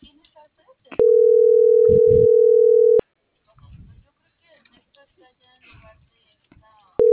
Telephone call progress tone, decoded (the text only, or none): ringback tone